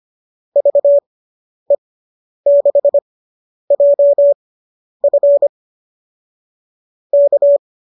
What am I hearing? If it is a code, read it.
Morse code, heard as a keyed tone, about 25 wpm, spaced as for 10 wpm, 570 Hz. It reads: VE6JF K